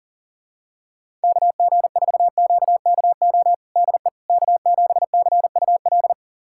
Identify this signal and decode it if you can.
Morse code, heard as a keyed tone, about 40 wpm, 700 Hz. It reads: KG4QKO DE K7CUL
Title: USA Callsign DE Callsign 40wpm